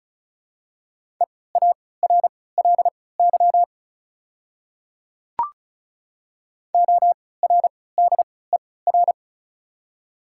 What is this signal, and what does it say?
Morse code, heard as a keyed tone, about 36 wpm, spaced as for 20 wpm, 690 Hz. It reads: EARLY ORDER
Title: 3x Character Spacing - Top 500 Words - 35wpm